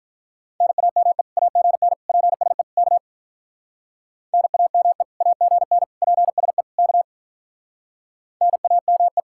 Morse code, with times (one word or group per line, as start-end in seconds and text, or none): 0.6-1.2 NAME
1.4-1.9 AGN
2.1-2.6 PSE
2.8-3.0 K
4.3-5.0 NAME
5.2-5.8 AGN
6.0-6.6 PSE
6.8-7.0 K
8.4-9.2 NAME